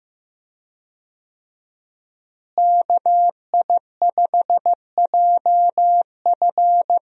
Morse code, heard as a keyed tone, about 15 wpm, 700 Hz. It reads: KI5JF
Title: USA Call Signs - 5 Characters - Repeated 3x - 15wpm